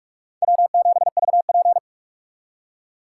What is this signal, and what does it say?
Morse code, heard as a keyed tone, about 46 wpm, 700 Hz. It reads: W7VP
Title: K1USN SST QSO Practice - 45wpm